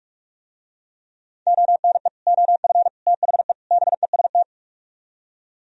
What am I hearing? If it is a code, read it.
Morse code, heard as a keyed tone, about 44 wpm, 700 Hz. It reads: ONE OF THE BEST